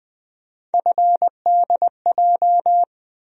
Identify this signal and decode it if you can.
Morse code, heard as a keyed tone, about 20 wpm, 700 Hz. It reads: FDJ
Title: USA Call Sign Postfixes - 3 Letters 20wpm